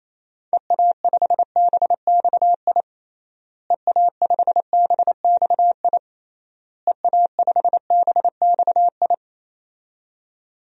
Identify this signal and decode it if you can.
Morse code, heard as a keyed tone, about 28 wpm, 700 Hz. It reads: EA5BXI EA5BXI EA5BXI
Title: DXCC-100 Call Signs - 6 Characters - Repeated 3x - 28wpm